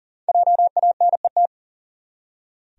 Morse code, heard as a keyed tone, about 39 wpm, 700 Hz. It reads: JANET
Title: USA Common Names - 40wpm